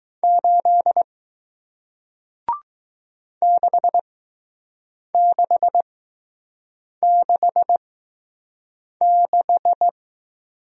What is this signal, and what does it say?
Morse code, heard as a keyed tone, about 20 wpm, spaced as for 6 wpm, 700 Hz. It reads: 8 6666